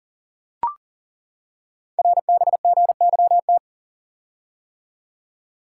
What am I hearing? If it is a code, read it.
Morse code, heard as a keyed tone, about 38 wpm, 700 Hz. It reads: RBGYT
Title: Code Groups - 5 Letters - 40wpm